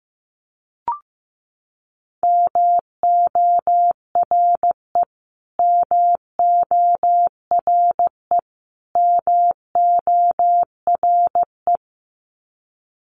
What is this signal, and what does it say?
Morse code, heard as a keyed tone, about 15 wpm, 700 Hz. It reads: MORE MORE MORE
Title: Top 300 Words Repeated 15wpm